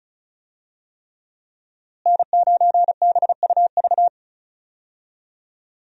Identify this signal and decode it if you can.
Morse code, heard as a keyed tone, about 34 wpm, 700 Hz. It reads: N9BUV
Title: USA Call Signs - 5 Characters - Repeated 3x - 35wpm